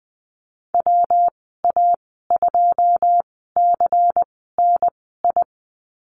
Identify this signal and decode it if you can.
Morse code, heard as a keyed tone, about 20 wpm, spaced as for 14 wpm, 700 Hz. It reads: WA2CNI
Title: 147 Lesson 24g - Letter y - Call Signs 20wpm